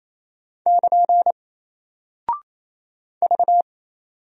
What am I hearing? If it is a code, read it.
Morse code, heard as a keyed tone, about 28 wpm, 700 Hz. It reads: <KN> V
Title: Single Character (Letter:Number:Punctuation:Prosign) Alt 28wpm